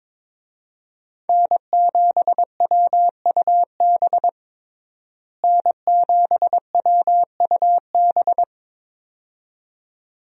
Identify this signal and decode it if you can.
Morse code, heard as a keyed tone, about 22 wpm, 700 Hz. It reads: N7WUB N7WUB
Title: USA Call Signs - 5 Characters - Repeated 2x - 22wpm